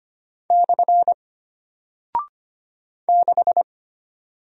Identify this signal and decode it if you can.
Morse code, heard as a keyed tone, about 25 wpm, 700 Hz. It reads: / 6
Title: Single Character (Letter:Number:Punctuation:Prosign) Alt 25wpm